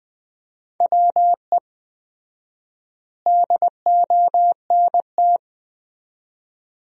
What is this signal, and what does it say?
Morse code, heard as a keyed tone, about 21 wpm, 700 Hz. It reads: WE DONT
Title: 4x Word-Spacing - Sentences from Top 300 Words - 20wpm